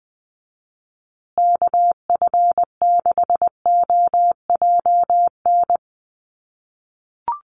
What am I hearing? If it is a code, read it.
Morse code, heard as a keyed tone, about 20 wpm, 700 Hz. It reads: KF6OJN
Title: USA Call Signs Repeated 2x 20wpm